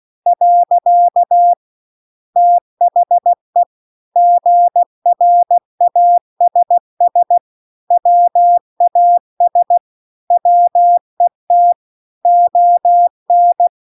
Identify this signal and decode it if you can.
Morse code, heard as a keyed tone, about 16 wpm, 700 Hz. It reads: . THE GRASS WAS WET ON
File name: CW-The-Big-Twoheaded-River-2-16WPM